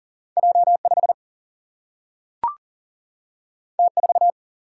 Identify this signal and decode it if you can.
Morse code, heard as a keyed tone, about 39 wpm, 700 Hz. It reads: J 5 T 4
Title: Single-Letter Single-Number 40wpm